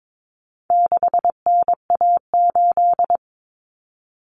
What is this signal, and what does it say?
Morse code, heard as a keyed tone, about 22 wpm, 700 Hz. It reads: 6NA8
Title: Common Cut Numbers - 4-Digit 22wpm